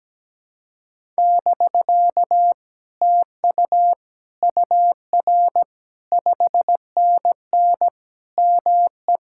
Morse code, heard as a keyed tone, about 17 wpm, 700 Hz. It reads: <BK> TU UR 5NN ME
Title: Speed-Racing Common QSO Phrases Encore - 17wpm